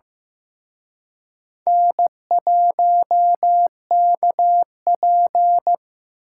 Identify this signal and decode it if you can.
Morse code, heard as a keyed tone, about 18 wpm, 700 Hz. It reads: N1KP
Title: Speed Racing - USA 1x2 Call Signs 15wpm